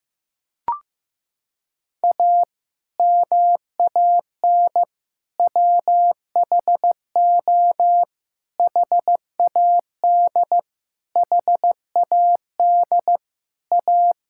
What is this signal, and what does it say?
Morse code, heard as a keyed tone, about 15 wpm, 700 Hz. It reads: A MAN WHO HAD HAD A
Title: Sentences from Top 1000 Words - Review Part 6 - 15wpm